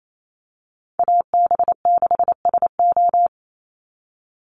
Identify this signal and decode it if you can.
Morse code, heard as a keyed tone, about 28 wpm, 700 Hz. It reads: AB6SO
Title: NA SOTA Chaser Callsigns 28wpm